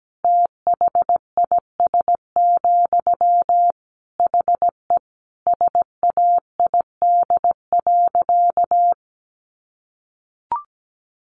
Morse code, heard as a keyed tone, about 17 wpm, 700 Hz. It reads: THIS, HE SAID.
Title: Sentences from Top 400 Words 17wpm